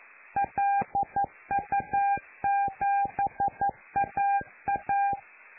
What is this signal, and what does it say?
Morse code, heard as a keyed tone, about 11 wpm, 800 Hz. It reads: LU7AA
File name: lu4aao / Globo Amsat